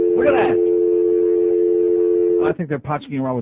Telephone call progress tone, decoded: dial tone